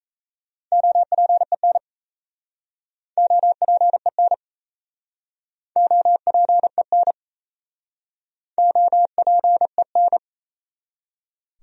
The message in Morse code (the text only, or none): OPEN OPEN OPEN OPEN